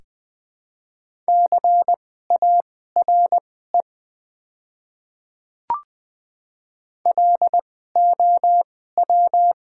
Morse code, heard as a keyed tone, about 20 wpm, spaced as for 14 wpm, 690 Hz. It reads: CARE LOW
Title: 2x Character Spacing - Top 300 Words - 20wpm